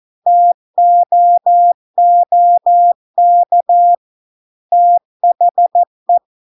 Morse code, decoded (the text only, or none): TOOK THE